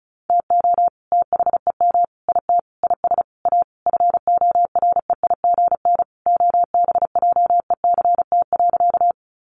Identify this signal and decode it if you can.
Morse code, heard as a keyed tone, about 34 wpm, 700 Hz. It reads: TO THEM IT IS A FOREIGN OBJECT.
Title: Sentences from Top 1400 Words 35wpm